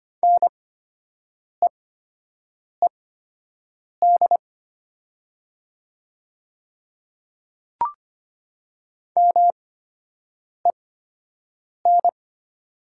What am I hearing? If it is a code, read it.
Morse code, heard as a keyed tone, about 25 wpm, spaced as for 7 wpm, 690 Hz. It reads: NEED MEN
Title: 8x Character Spacing - Top 500 Words - 25wpm